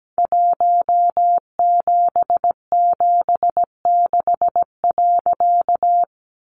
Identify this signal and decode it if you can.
Morse code, heard as a keyed tone, about 17 wpm, 700 Hz. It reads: 1776.
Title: A Sound Of Thunder 17wpm